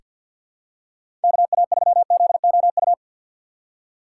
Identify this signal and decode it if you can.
Morse code, heard as a keyed tone, about 46 wpm, 700 Hz. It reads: KA3ZOU
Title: USA Call Signs - 6 Characters - 3x Character Spacing - 50wpm